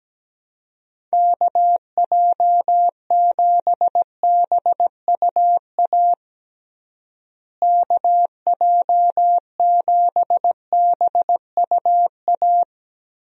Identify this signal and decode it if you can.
Morse code, heard as a keyed tone, about 17 wpm, 700 Hz. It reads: KJ7BUA KJ7BUA
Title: USA Call Signs - 6 Characters - Repeated 3x - 17wpm